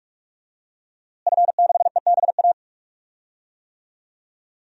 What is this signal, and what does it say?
Morse code, heard as a keyed tone, about 41 wpm, 700 Hz. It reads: F6EBA
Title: DXCC-100 Call Signs - 5 Characters - Repeated 3x - 45wpm